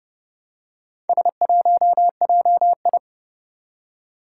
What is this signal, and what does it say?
Morse code, heard as a keyed tone, about 30 wpm, 700 Hz. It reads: S1JI
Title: Four Challenging Letter-Number Combinations 30wpm